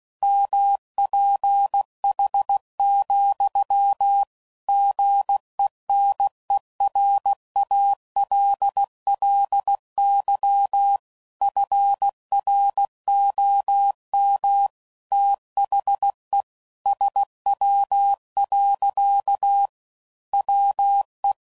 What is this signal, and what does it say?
Morse code, heard as a keyed tone, about 16 wpm, 800 Hz. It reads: MPH, GENERALLY FROM THE SW. WE
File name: CW Code Groups @15WPM